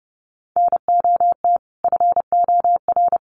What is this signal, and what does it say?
Morse code, heard as a keyed tone, about 30 wpm, 700 Hz. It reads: NOT FOR
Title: Sentences from Top 1300 Words 30wpm